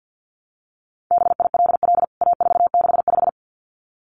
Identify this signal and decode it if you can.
Morse code, heard as a keyed tone, about 49 wpm, 700 Hz. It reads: BILL A465
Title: CWT Exchange 50wpm